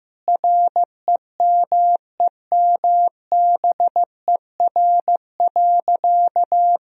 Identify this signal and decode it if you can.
Morse code, heard as a keyed tone, about 16 wpm, 700 Hz. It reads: REMEMBER.
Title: Sentences from Top 1000 Words - Review Part 2 - 15wpm